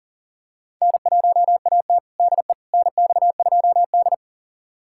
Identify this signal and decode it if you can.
Morse code, heard as a keyed tone, about 40 wpm, 700 Hz. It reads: N1AT DE NX2D
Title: USA Callsign DE Callsign - 4 Characters - 40wpm